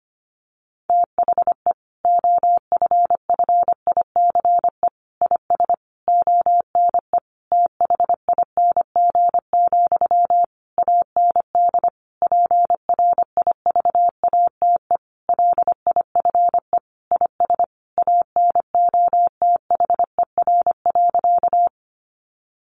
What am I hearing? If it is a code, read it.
Morse code, heard as a keyed tone, about 25 wpm, 700 Hz. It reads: THE OFFICE IS ONE THING, AND PRIVATE LIFE IS ANOTHER.
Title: Sentences from Top 1000 Words 25wpm